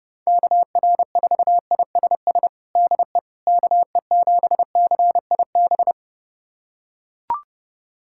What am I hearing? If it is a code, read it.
Morse code, heard as a keyed tone, about 32 wpm, 690 Hz. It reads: KR4ISS DE KE7CIB
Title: USA Callsign DE Callsign - 6 Characters - 30wpm